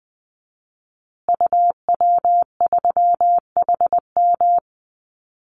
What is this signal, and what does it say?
Morse code, heard as a keyed tone, about 20 wpm, 700 Hz. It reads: UW3HM
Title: Call Signs Repeated 3x - 20wpm